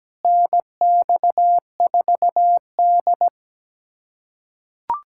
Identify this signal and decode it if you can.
Morse code, heard as a keyed tone, about 17 wpm, 700 Hz. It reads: NX4D
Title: USA Callsign DE Callsign - 4 Characters - 17wpm